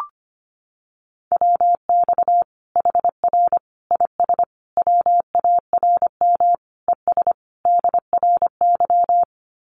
Morse code, heard as a keyed tone, about 25 wpm, 700 Hz. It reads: WX HR IS WARM ES DRY
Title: Common QSO Phrases 25wpm